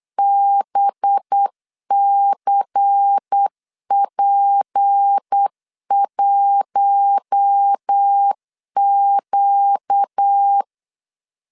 Morse code, heard as a keyed tone, about 8 wpm, 800 Hz. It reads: BCP1Q
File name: LEZIONI / 6 PKX0 / 6 PKX0 8wpm 3341